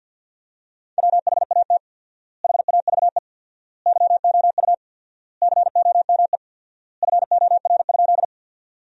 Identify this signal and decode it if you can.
Morse code, heard as a keyed tone, about 50 wpm, 700 Hz. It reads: WHAT HAVE YOU COME FOR?
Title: 4x Word-Spacing - Sentences from Top 100 Words - 50wpm